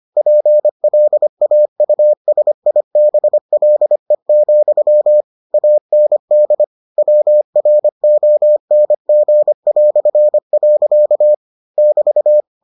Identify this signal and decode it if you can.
Morse code, heard as a keyed tone, about 25 wpm, 600 Hz. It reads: PLAUSIBLE, AND WRONG". <BT>